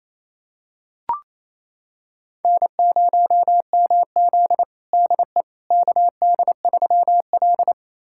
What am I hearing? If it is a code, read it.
Morse code, heard as a keyed tone, about 28 wpm, 700 Hz. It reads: N0MZ DE KD3L
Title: USA Callsign DE Callsign - 4 Characters - 28wpm